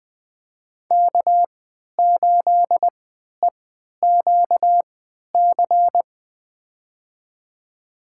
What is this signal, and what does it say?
Morse code, heard as a keyed tone, about 20 wpm, spaced as for 11 wpm, 700 Hz. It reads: K8EQC
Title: USA Call Signs - 5 Characters - 3x Character Spacing - 20wpm